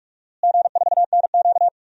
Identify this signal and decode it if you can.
Morse code, heard as a keyed tone, about 43 wpm, 700 Hz. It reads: G4NQ
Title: DXCC-100 Call Signs - 4 Characters - Repeated 2x - 45wpm